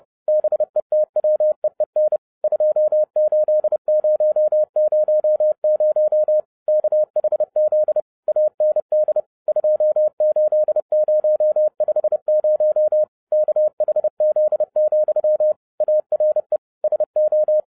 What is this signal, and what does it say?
Morse code, heard as a keyed tone, about 30 wpm, 610 Hz. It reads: BETWEEN 28000 KHZ AND 28050 KHZ, ARE SO
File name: 2012-01-mp3 / n12-I DIDNT MISS THE SOLAR CYCLE 24-30wpm